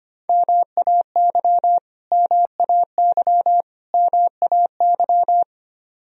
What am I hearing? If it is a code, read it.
Morse code, heard as a keyed tone, about 25 wpm, 700 Hz. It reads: MAY MAY MAY
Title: Top 100 Words Repeated 25wpm